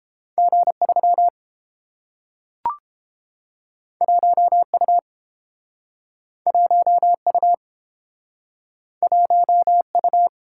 Morse code, heard as a keyed tone, about 30 wpm, 700 Hz. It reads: G3 1U 1U 1U